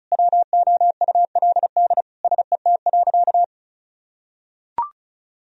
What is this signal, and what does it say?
Morse code, heard as a keyed tone, about 36 wpm, 700 Hz. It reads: WOULD SET.
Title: Sentences from Top 1000 Words - Review Part 1 - 35wpm